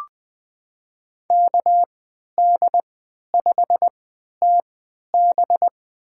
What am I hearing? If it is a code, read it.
Morse code, heard as a keyed tone, about 20 wpm, spaced as for 11 wpm, 700 Hz. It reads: KD5TB
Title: USA Call Signs - 5 Characters - 3x Character Spacing - 20wpm